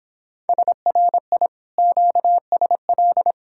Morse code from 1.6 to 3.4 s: QSL

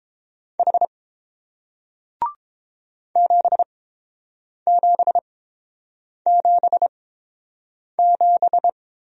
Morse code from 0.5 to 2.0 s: H

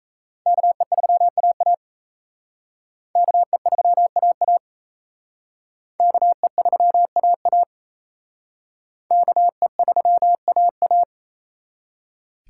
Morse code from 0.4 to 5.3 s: KE3AA KE3AA

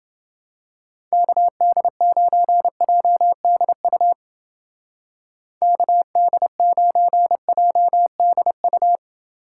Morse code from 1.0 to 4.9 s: KD9JDU

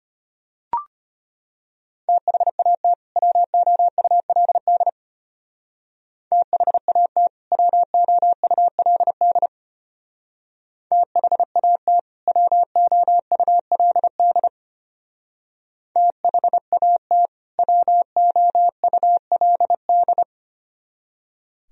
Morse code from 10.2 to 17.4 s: THAT WOULD THAT

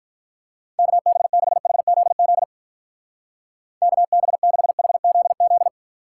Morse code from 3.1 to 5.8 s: KB6HZZ

